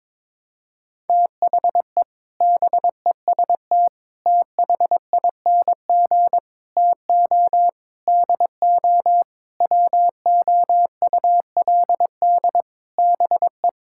Morse code, heard as a keyed tone, about 22 wpm, 700 Hz. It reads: THE BEST THING TO DO WOULD BE
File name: practice / 22wpm / Sets of 8 Words - 22wpm